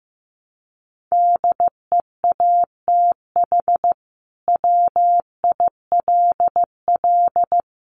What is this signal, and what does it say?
Morse code, heard as a keyed tone, about 15 wpm, 700 Hz. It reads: DEATH WILL
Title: Sentences from Top 1100 Words 15wpm